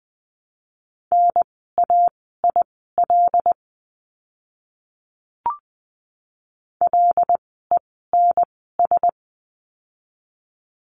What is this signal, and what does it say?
Morse code, heard as a keyed tone, about 20 wpm, spaced as for 14 wpm, 700 Hz. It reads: NAIL LENS
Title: 075 Lesson 13f - Letter L - Words 20wpm